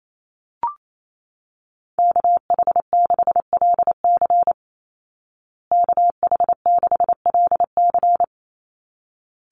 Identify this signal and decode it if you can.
Morse code, heard as a keyed tone, about 28 wpm, 700 Hz. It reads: KH6LC KH6LC
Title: Call Signs Repeated 3x - 28wpm